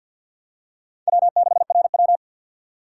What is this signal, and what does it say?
Morse code, heard as a keyed tone, about 53 wpm, 700 Hz. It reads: W6RW
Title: USA Call Signs - 4 Characters - 3x Character Spacing - 50wpm